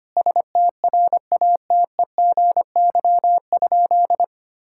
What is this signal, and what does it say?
Morse code, heard as a keyed tone, about 25 wpm, 700 Hz. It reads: STRATEGY?
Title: Sentences from Top 1000 Words - Review Part 5 - 25wpm